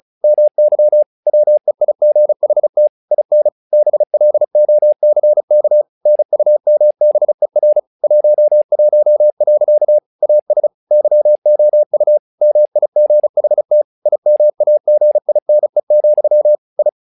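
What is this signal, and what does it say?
Morse code, heard as a keyed tone, about 35 wpm, 600 Hz. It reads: MY WEIGHT IN BLOCK NUMBER 11. AS YOU MIGHT IMAGINE, I
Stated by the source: Morse practice text "Tower Trouble" sent at 35 WPM, from the SKCC - Straight Key Century Club learning center audio files.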